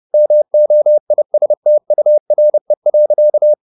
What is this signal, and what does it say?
Morse code, heard as a keyed tone, about 30 wpm, 600 Hz. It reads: MOISTURE.